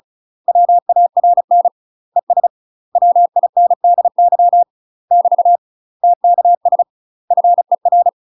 Morse code, heard as a keyed tone, about 34 wpm, 700 Hz. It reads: WARN ES WINDY <BT> TKS FER
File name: UK MASTER QSOs TRAINING / 35WPM / UK MASTER QSOs 0032